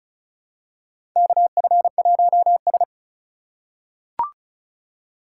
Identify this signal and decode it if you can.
Morse code, heard as a keyed tone, about 36 wpm, 700 Hz. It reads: KF1S
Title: USA Call Signs - 4 Characters - 4x Character Spacing - 35wpm